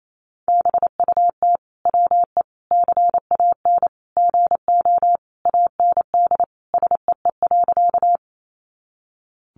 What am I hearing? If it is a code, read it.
Morse code, heard as a keyed tone, about 28 wpm, 700 Hz. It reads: BUT WE CAN GO AND SEE.
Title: Sentences from Top 100 Words 28wpm